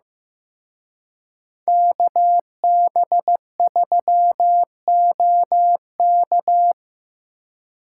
Morse code, heard as a keyed tone, about 18 wpm, 700 Hz. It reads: KB3OK
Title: Speed Racing - USA 2x2 Call Signs 15wpm